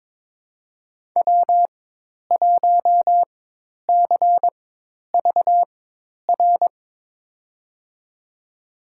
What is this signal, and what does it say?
Morse code, heard as a keyed tone, about 22 wpm, spaced as for 10 wpm, 700 Hz. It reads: W1CVR
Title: USA Call Signs - 5 Characters - 4x Character Spacing - 22wpm